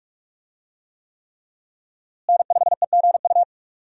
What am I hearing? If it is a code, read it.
Morse code, heard as a keyed tone, about 46 wpm, 700 Hz. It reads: N5EGU
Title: USA Call Signs - 5 Characters - Repeated 3x - 45wpm